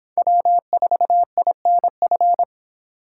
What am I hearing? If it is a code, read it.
Morse code, heard as a keyed tone, about 26 wpm, 700 Hz. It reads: W4INF
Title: Speed Racing - USA 1x3 Call Signs 22wpm